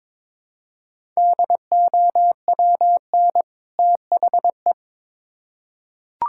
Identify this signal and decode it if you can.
Morse code, heard as a keyed tone, about 21 wpm, 700 Hz. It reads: DOWN THE
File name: Build A Phrase - 2-Word Phrases - 22wpm